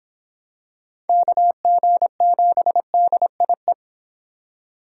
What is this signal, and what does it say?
Morse code, heard as a keyed tone, about 26 wpm, 700 Hz. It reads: KG7DIE